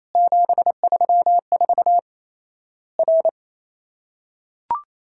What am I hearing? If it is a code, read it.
Morse code, heard as a keyed tone, about 28 wpm, 700 Hz. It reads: 734 R
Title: Worked All Europe DX Contest - QTC Exchange - 28wpm